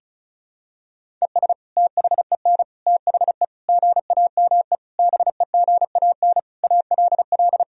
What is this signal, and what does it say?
Morse code, heard as a keyed tone, about 36 wpm, 700 Hz. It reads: ES THEN THE GAME BEGAN ALL
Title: Sentences from Top 500 Words and Abbreviations 35wpm